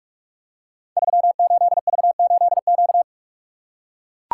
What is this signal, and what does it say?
Morse code, heard as a keyed tone, about 44 wpm, 700 Hz. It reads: 38V8Q
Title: Code Groups - 5 Letter:Numbers - 45wpm